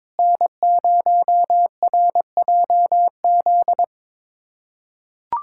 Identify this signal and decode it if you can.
Morse code, heard as a keyed tone, about 22 wpm, 700 Hz. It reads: N0RJZ
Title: USA Callsign DE Callsign - 5 Characters - 22wpm